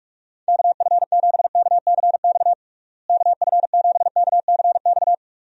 Morse code, heard as a keyed tone, about 46 wpm, 700 Hz. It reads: KF7KCX KF7KCX
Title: USA Call Signs - 6 Characters - Repeated 3x - 45wpm